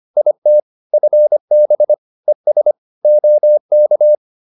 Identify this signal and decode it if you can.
Morse code, heard as a keyed tone, about 25 wpm, 600 Hz. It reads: IT FB ES OK